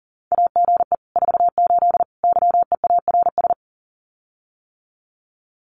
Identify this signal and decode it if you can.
Morse code, heard as a keyed tone, about 40 wpm, 700 Hz. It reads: AGE 48 YEARS
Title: Common QSO Phrases Encore 40wpm